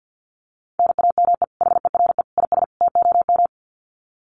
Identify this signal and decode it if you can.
Morse code, heard as a keyed tone, about 50 wpm, 700 Hz. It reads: NAME HERE IS TOM
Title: Common QSO Phrases 50wpm